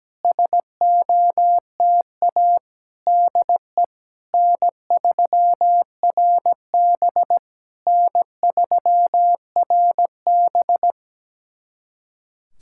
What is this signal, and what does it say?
Morse code, heard as a keyed tone, about 17 wpm, 700 Hz. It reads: SOTA DE N3RB N3RB
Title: SOTA QSO Practice 17wpm